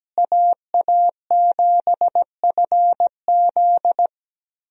Morse code, heard as a keyed tone, about 17 wpm, 700 Hz. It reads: AA7FZ
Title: USA Call Signs - 5 Characters - Repeated 2x - 17wpm